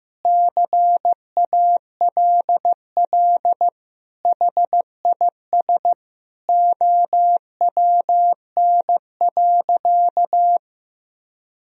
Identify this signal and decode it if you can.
Morse code, heard as a keyed tone, about 15 wpm, 700 Hz. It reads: CALL HIS OWN.
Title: Sentences from Top 1000 Words - Review Part 2 - 15wpm